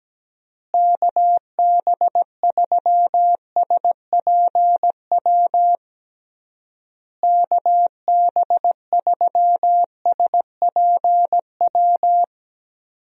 Morse code, heard as a keyed tone, about 17 wpm, 700 Hz. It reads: KB3SPW KB3SPW
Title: USA Call Signs - 6 Characters - Repeated 3x - 17wpm